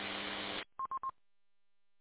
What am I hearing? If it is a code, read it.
There is no code.